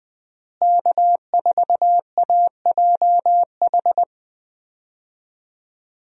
Morse code, heard as a keyed tone, about 20 wpm, 700 Hz. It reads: K4AJH
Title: USA Call Signs - 5 Characters - Repeated 2x - 20wpm